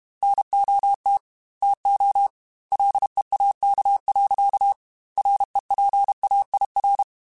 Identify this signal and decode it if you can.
Morse code, heard as a keyed tone, about 32 wpm, 800 Hz. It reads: NOT TO LEAK. REPAIR
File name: Quotes / Quotes-mp3s / Airtight @30